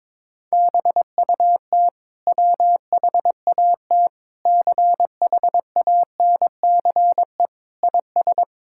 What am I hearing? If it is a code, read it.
Morse code, heard as a keyed tone, about 21 wpm, 700 Hz. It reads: BUT WHAT CHANCE IS
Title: Sentences from Top 1000 Words - Review Part 6 - 22wpm